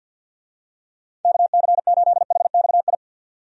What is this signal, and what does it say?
Morse code, heard as a keyed tone, about 47 wpm, 700 Hz. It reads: KC8HXI
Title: USA Call Signs - 6 Characters - 50wpm